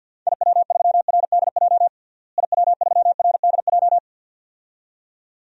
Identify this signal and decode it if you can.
Morse code, heard as a keyed tone, about 50 wpm, 700 Hz. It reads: IW3RDJ IW3RDJ
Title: DXCC-100 Call Signs - 6 Characters - Repeated 2x - 50wpm